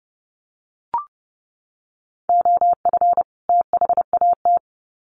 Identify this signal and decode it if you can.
Morse code, heard as a keyed tone, about 30 wpm, 700 Hz. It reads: OF THAT
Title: Sets of 2 Words 30wpm